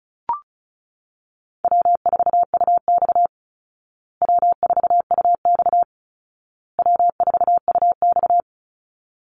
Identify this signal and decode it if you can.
Morse code, heard as a keyed tone, about 34 wpm, 700 Hz. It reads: W4UX W4UX W4UX
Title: Easy Call Signs Repeated 3x - 35wpm